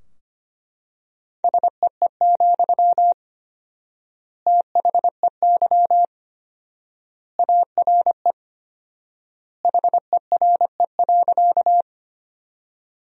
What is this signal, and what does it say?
Morse code, heard as a keyed tone, about 25 wpm, 700 Hz. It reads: SEE, THEY ARE HERE.
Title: 4x Word-Spacing - Sentences from Top 200 Words - 25wpm